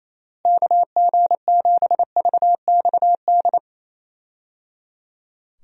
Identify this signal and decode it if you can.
Morse code, heard as a keyed tone, about 28 wpm, 700 Hz. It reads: KG7VXD
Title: USA Call Signs - 6 Characters - Repeated 3x - 28wpm